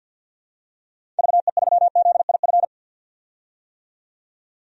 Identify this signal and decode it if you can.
Morse code, heard as a keyed tone, about 52 wpm, 700 Hz. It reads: VE3ZIF
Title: DXCC-100 Call Signs - 6 Characters - 50wpm